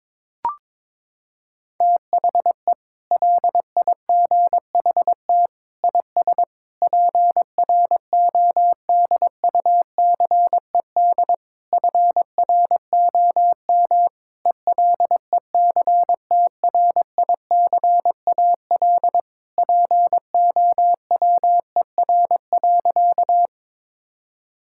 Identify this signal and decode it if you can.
Morse code, heard as a keyed tone, about 22 wpm, 700 Hz. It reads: THE LIGHT IS PRODUCED FROM ELECTRICAL POWER.
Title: Sentences from Top 1000 Words - Review Part 6 - 22wpm